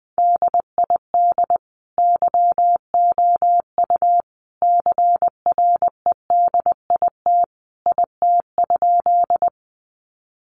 Morse code, heard as a keyed tone, about 20 wpm, 700 Hz. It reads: DID YOU CREDIT IT?
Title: Sentences from Top 1000 Words 20wpm